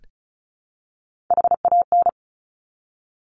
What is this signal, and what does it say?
Morse code, heard as a keyed tone, about 36 wpm, 700 Hz. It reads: HAN
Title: Top Three Letter Combinations in Words 35wpm